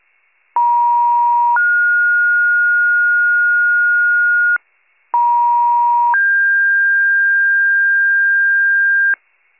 Two-tone page, heard as a one-second tone, then a three-second tone, about 950 then 1430 Hz, about 950 then 1600 Hz, with noise. It is heard from 0.6 to 9.1 s.